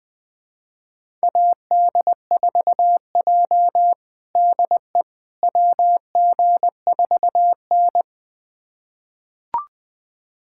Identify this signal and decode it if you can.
Morse code, heard as a keyed tone, about 21 wpm, 700 Hz. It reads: AD4J DE WG4N
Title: USA Callsign DE Callsign - 4 Characters - 20wpm